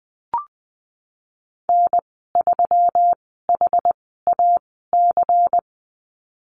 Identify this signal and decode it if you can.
Morse code, heard as a keyed tone, about 20 wpm, spaced as for 15 wpm, 700 Hz. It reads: N3HAC
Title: 147 Lesson 24g - Letter y - Call Signs 20wpm